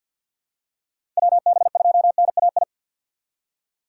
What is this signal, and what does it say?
Morse code, heard as a keyed tone, about 53 wpm, 700 Hz. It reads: WB2NAI
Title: USA Call Signs - 6 Characters - 6x Character Spacing - 50wpm